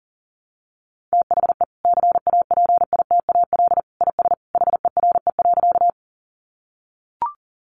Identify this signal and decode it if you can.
Morse code, heard as a keyed tone, about 40 wpm, 700 Hz. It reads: THE CAPITAL IS HERE.
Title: Sentences from Top 1000 Words 40wpm